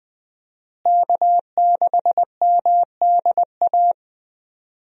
Morse code, heard as a keyed tone, about 20 wpm, 700 Hz. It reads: K6MDA
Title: USA Call Signs - 5 Characters - Repeated 3x - 20wpm